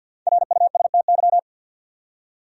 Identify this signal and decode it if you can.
Morse code, heard as a keyed tone, about 52 wpm, 700 Hz. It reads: RUSTY
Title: USA Common Names - Max 5 Letters - 50wpm